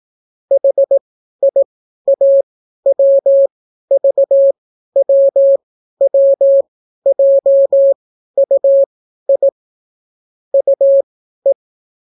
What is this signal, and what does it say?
Morse code, heard as a keyed tone, about 17 wpm, spaced as for 12 wpm, 550 Hz. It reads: HIAWVWWJUI UE